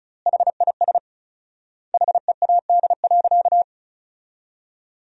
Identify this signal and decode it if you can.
Morse code, heard as a keyed tone, about 36 wpm, 700 Hz. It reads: HIS HEAD.
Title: 4x Word-Spacing - Sentences from Top 300 Words - 35wpm